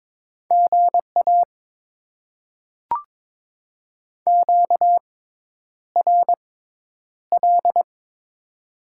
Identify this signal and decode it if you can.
Morse code, heard as a keyed tone, about 22 wpm, spaced as for 10 wpm, 700 Hz. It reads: GA QRL
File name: Top 20 QSO Elements - 6x Character Spacing - 22wpm